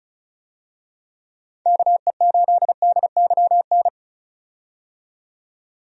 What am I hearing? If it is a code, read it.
Morse code, heard as a keyed tone, about 34 wpm, 700 Hz. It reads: KE8DYN
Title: USA Call Signs - 6 Characters - 35wpm